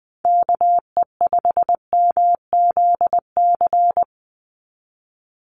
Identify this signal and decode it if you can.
Morse code, heard as a keyed tone, about 20 wpm, 700 Hz. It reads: KE5MZC
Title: USA Call Signs 20wpm